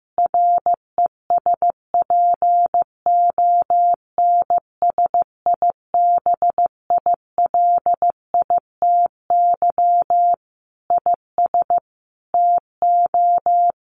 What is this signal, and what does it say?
Morse code, heard as a keyed tone, about 15 wpm, 700 Hz. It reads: RESPONSIBILITY IS TO
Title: Sentences from Top 1200 Words 15wpm